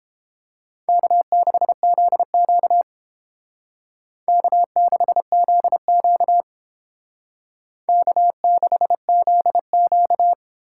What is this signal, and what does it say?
Morse code, heard as a keyed tone, about 30 wpm, 700 Hz. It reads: K6ZQ K6ZQ K6ZQ